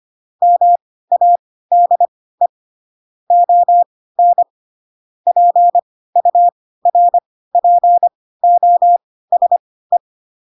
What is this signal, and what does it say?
Morse code, heard as a keyed tone, about 25 wpm, spaced as for 16 wpm, 700 Hz. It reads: MADE ON PURPOSE
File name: Pinocchio / Pinocchio ch3 10 25 ik1hge